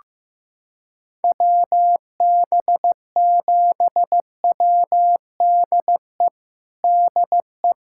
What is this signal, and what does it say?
Morse code, heard as a keyed tone, about 14 wpm, 700 Hz. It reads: WB7WDE DE
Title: USA Callsign DE Callsign - 6 Characters - 15wpm